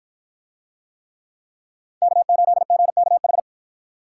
Morse code, heard as a keyed tone, about 51 wpm, 700 Hz. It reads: K8GKH